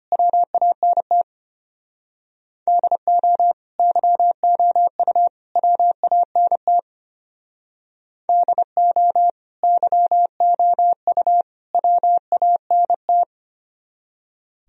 Morse code, heard as a keyed tone, about 31 wpm, 700 Hz. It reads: WANT DO YOU WANT DO YOU WANT